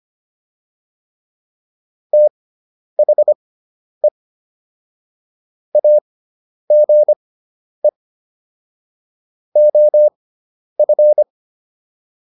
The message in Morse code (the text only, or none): THE AGE OF